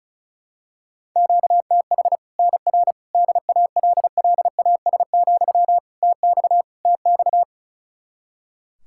Morse code, heard as a keyed tone, about 36 wpm, 700 Hz. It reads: QTH NR DALLAS, TX TX
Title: Practice QSO - Exchange 1 - Beginning - 35wpm